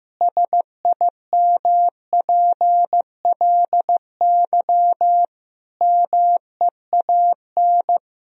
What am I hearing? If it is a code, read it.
Morse code, heard as a keyed tone, about 15 wpm, 700 Hz. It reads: SIMPLY MEAN
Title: Sentences from Top 1000 Words - Review Part 4 - 15wpm